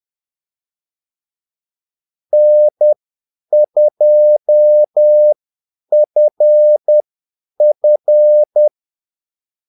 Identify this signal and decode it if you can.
Morse code, heard as a keyed tone, about 10 wpm, spaced as for 8 wpm, 600 Hz. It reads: N2FF
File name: Callsigns-10-8wpm